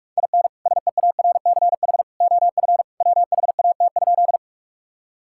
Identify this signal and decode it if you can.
Morse code, heard as a keyed tone, about 46 wpm, 700 Hz. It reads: IN SEARCH OF WHAT?
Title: Sentences from Top 1000 Words - Review Part 5 - 45wpm